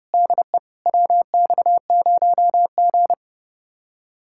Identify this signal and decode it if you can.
Morse code, heard as a keyed tone, about 36 wpm, 700 Hz. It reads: DE WX0G
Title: Speed-Racing Common QSO Phrases Encore - 30wpm